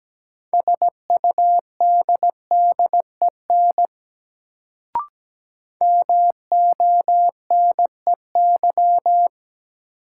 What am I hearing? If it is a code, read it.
Morse code, heard as a keyed tone, about 17 wpm, 700 Hz. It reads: SUDDEN MONEY
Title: Sets of 4 Words - New Word Drill - 17wpm